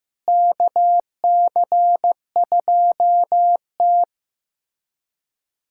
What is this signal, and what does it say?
Morse code, heard as a keyed tone, about 15 wpm, 700 Hz. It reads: KC2T
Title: USA Callsign DE Callsign - 4 Characters - 15wpm